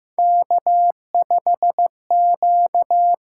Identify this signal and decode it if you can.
Morse code, heard as a keyed tone, about 15 wpm, 700 Hz. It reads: K5Q
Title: USA Call Signs - 6x Character Spacing - 15wpm